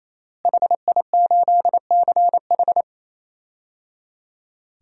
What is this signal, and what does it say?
Morse code, heard as a keyed tone, about 28 wpm, 700 Hz. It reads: HI8CH
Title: DXCC-100 Call Signs - 5 Characters - Repeated 3x - 28wpm